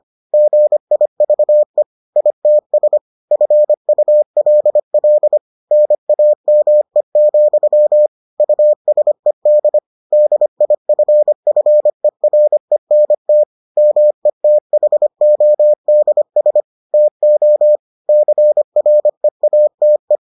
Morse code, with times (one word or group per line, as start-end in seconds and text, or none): 0.3-1.8 GIVE
2.2-3.0 ITS
3.3-5.4 FULL
5.7-8.1 NAME,
8.4-9.8 USED
10.1-13.4 DIFFERENT
13.8-16.6 METHODS
16.9-17.8 TO
18.1-20.2 CREATE